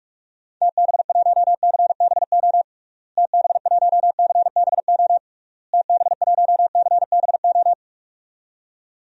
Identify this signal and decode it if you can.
Morse code, heard as a keyed tone, about 41 wpm, 700 Hz. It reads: TB1CBO TB1CBO TB1CBO
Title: DXCC-100 Call Signs - 6 Characters - Repeated 3x - 45wpm